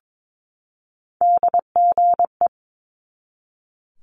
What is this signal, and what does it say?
Morse code, heard as a keyed tone, about 22 wpm, 700 Hz. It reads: DGE
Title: Multi-letter Phonemes 22wpm